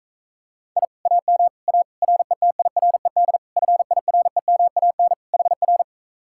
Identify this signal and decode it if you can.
Morse code, heard as a keyed tone, about 44 wpm, 700 Hz. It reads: I AM A RETIRED FIREMAN HR